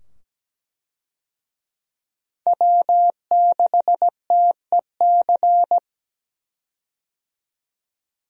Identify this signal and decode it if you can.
Morse code, heard as a keyed tone, about 17 wpm, 700 Hz. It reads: W6TEC
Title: USA Call Signs - 5 Characters - 17wpm